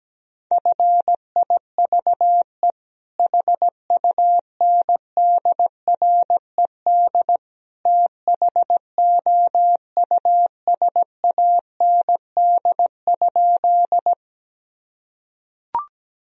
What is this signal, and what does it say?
Morse code, heard as a keyed tone, about 17 wpm, 700 Hz. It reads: FIVE HUNDRED THOUSAND?
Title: Sentences from Top 1000 Words - Review Part 3 - 17wpm